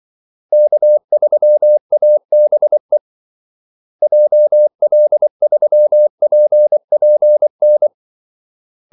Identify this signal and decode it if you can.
Morse code, heard as a keyed tone, about 24 wpm, 600 Hz. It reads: K3ABE JL3PPN
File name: Callsigns-24wpm